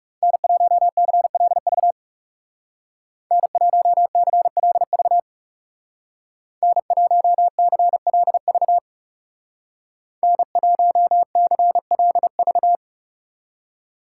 Morse code, heard as a keyed tone, about 36 wpm, 700 Hz. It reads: N1CLV N1CLV N1CLV N1CLV
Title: Speed Racing - USA 1x3 Call Signs 30wpm